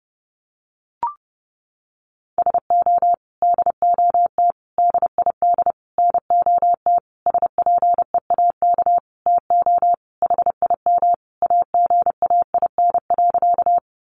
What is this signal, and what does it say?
Morse code, heard as a keyed tone, about 30 wpm, 700 Hz. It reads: SO DOT DID NOT SPEAK TO HIM AGAIN.
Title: Sentences from Top 1500 Words 30wpm